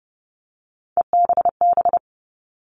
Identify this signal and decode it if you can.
Morse code, heard as a keyed tone, about 30 wpm, 700 Hz. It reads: EBB
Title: All Cut Numbers - 3-Digit 30wpm